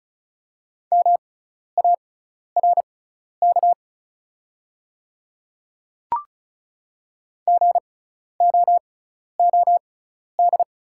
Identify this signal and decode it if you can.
Morse code, heard as a keyed tone, about 36 wpm, spaced as for 12 wpm, 700 Hz. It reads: MARK GOOD
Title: 6x Character Spacing - Top 300 Words - 35wpm